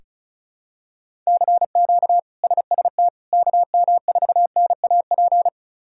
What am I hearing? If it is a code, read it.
Morse code, heard as a keyed tone, about 36 wpm, 700 Hz. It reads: CQ SST KM4NAP
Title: K1USN SST QSO Practice - 35wpm